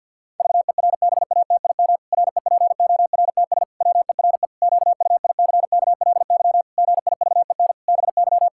Morse code, heard as a keyed tone, about 50 wpm, 700 Hz. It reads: VERBATIM REPORTS WERE QUICKLY GIVEN BY
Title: Pangram Challenge v2 - 50wpm